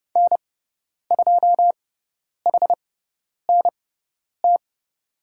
Morse code, heard as a keyed tone, about 30 wpm, spaced as for 10 wpm, 700 Hz. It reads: N2HNT
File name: Lesson 17g - Letter u - Call Signs 30wpm